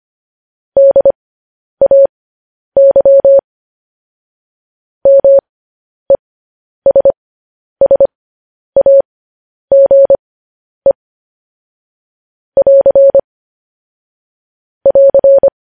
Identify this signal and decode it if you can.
Morse code, heard as a keyed tone, about 25 wpm, spaced as for 10 wpm, 550 Hz. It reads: DAY MESSAGE <AR> <AR>
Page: news-headlines-in-morse-code-at-10-wpm